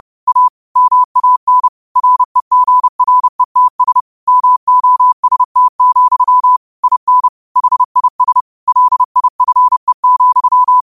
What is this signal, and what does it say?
Morse code, heard as a keyed tone, about 30 wpm, 1000 Hz. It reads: A MAN REGRETS MOST, IN HIS LIFE,